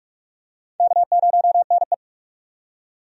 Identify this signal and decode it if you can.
Morse code, heard as a keyed tone, about 46 wpm, 700 Hz. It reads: K0NE
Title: USA Call Signs - 4 Characters - 45wpm